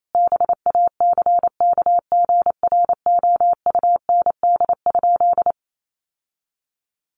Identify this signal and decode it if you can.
Morse code, heard as a keyed tone, about 28 wpm, 700 Hz. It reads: BACKGROUND?
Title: Sentences from Top 1300 Words 28wpm